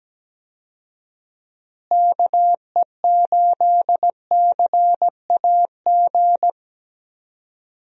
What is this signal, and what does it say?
Morse code, heard as a keyed tone, about 17 wpm, 700 Hz. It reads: KE8CAG
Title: USA Call Signs - 6 Characters - Repeated 2x - 17wpm